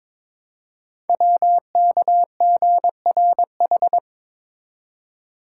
Morse code, heard as a keyed tone, about 23 wpm, 700 Hz. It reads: WKGRH
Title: Code Groups - 5 Letters - 22wpm